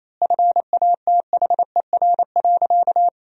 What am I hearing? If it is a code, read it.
Morse code, heard as a keyed tone, about 28 wpm, 700 Hz. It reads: FATHER.
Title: 2x Word-Spacing - Sentences from Top 400 Words - 28wpm